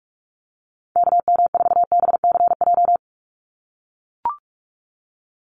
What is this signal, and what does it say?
Morse code, heard as a keyed tone, about 44 wpm, 700 Hz. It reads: KM4BCJ
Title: USA Call Signs 45wpm